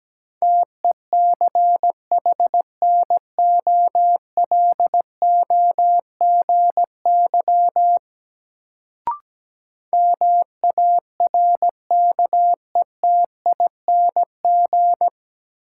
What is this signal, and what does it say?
Morse code, heard as a keyed tone, about 17 wpm, 700 Hz. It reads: TECHNOLOGY MARKETING
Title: Sentences from Top 1000 Words - Review Part 5 - New Word Drill - 17wpm